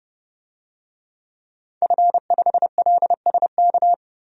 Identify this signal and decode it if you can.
Morse code, heard as a keyed tone, about 32 wpm, 700 Hz. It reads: F5LSK
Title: DXCC-100 Call Signs - 5 Characters - 30wpm